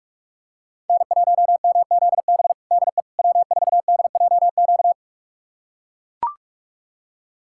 Morse code, heard as a keyed tone, about 41 wpm, 700 Hz. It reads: N1MZB DE W4DJQ
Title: USA Callsign DE Callsign - 5 Characters - 45wpm